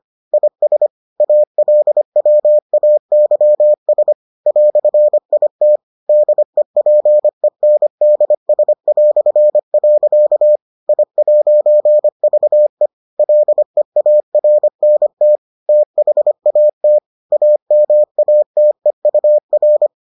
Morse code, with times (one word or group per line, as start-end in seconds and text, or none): 0.3-0.9 IS
1.2-4.1 ALWAYS
4.5-5.8 "IT
6.1-10.6 DEPENDS".
10.9-12.9 I'VE
13.2-15.4 LEARNT
15.7-17.0 THAT
17.3-19.9 AMATEUR